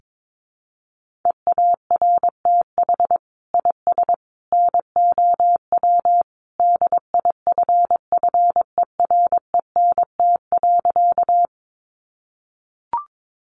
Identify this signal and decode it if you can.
Morse code, heard as a keyed tone, about 22 wpm, 700 Hz. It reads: EARTH IS NOW DIFFERENT.
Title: Sentences from Top 600 Words 22wpm